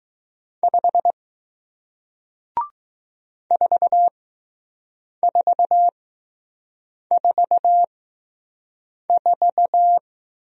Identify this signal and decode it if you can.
Morse code, heard as a keyed tone, about 20 wpm, spaced as for 6 wpm, 700 Hz. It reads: 5 4444